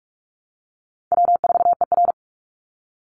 Morse code, heard as a keyed tone, about 44 wpm, 700 Hz. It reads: W4ER
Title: Easy Call Signs - 45wpm